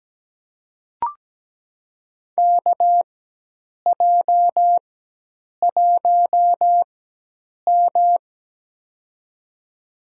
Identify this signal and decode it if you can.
Morse code, heard as a keyed tone, about 17 wpm, spaced as for 8 wpm, 700 Hz. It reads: KJ1M
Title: USA Call Signs - 4 Characters - 4x Character Spacing - 17wpm